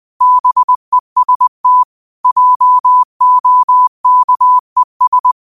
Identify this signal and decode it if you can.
Morse code, heard as a keyed tone, about 20 wpm, 1000 Hz. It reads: BEST JOKES